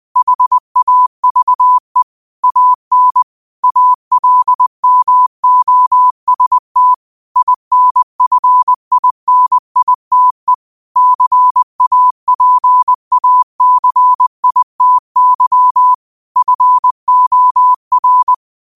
Morse code, heard as a keyed tone, about 20 wpm, 1000 Hz. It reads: HAVE AN ALMOST INFINITE CAPACITY FOR
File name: qotdfiles / 20wpm